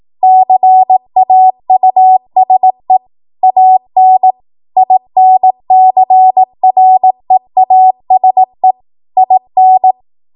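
Morse code, heard as a keyed tone, about 18 wpm, 750 Hz. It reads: CAUSE AN INCREASE IN
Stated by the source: ARRL 18 wpm Morse code practice archive transmission.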